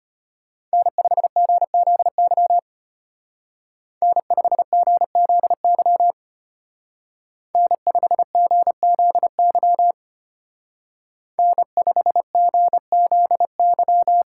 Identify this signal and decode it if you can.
Morse code, heard as a keyed tone, about 33 wpm, 700 Hz. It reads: N5GZY N5GZY N5GZY N5GZY